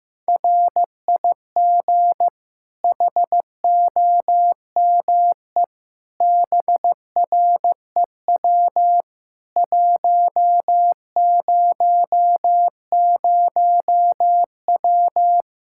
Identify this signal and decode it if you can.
Morse code, heard as a keyed tone, about 15 wpm, 700 Hz. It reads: RIG HOME BREW 100W
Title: K1USN SST QSO Practice 15wpm